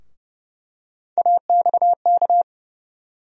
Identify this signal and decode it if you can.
Morse code, heard as a keyed tone, about 30 wpm, 700 Hz. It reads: AXK
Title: USA Call Sign Postfixes - 3 Letters 30wpm